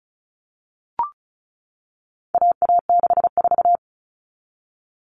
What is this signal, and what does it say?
Morse code, heard as a keyed tone, about 34 wpm, 700 Hz. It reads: AA64
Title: All Cut Numbers - 4-Digit 35wpm